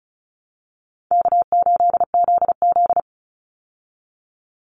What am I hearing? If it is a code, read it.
Morse code, heard as a keyed tone, about 34 wpm, 700 Hz. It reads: K8ZZ
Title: Easy Call Signs Repeated 2x - 35wpm